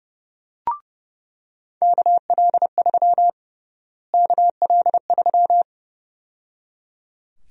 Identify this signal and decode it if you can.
Morse code, heard as a keyed tone, about 30 wpm, 700 Hz. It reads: KL3 KL3
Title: USA Call Sign Prefixes - Repeated 2x - 30wpm